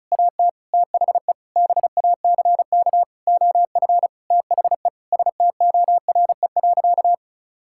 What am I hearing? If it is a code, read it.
Morse code, heard as a keyed tone, about 34 wpm, 700 Hz. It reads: AT THE BACK OF THE STORE.
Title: Sentences from Top 1000 Words - Review Part 5 - 35wpm